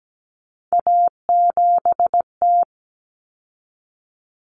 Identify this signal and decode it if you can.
Morse code, heard as a keyed tone, about 17 wpm, 700 Hz. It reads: A7T
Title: Common Cut Numbers - 3-Digit 17wpm